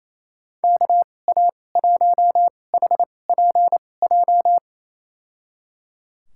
Morse code, heard as a keyed tone, about 28 wpm, spaced as for 20 wpm, 700 Hz. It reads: KA1HPJ
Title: USA Call Signs - 6 Characters - 2x Character Spacing - 28wpm